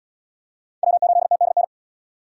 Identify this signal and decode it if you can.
Morse code, heard as a keyed tone, about 74 wpm, 700 Hz. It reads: W8TDN